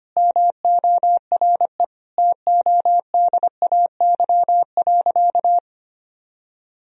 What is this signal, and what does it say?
Morse code, heard as a keyed tone, about 25 wpm, 700 Hz. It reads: MORE TODAY.
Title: Sentences from Top 1000 Words - Review Part 4 - 25wpm